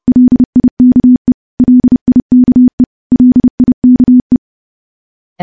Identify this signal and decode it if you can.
Morse code, heard as a keyed tone, about 30 wpm, 250 Hz. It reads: LIKE LIKE LIKE
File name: morse-training-30WPM